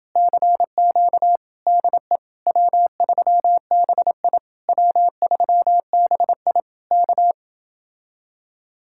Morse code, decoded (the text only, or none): CQ DE W3BI W3BI K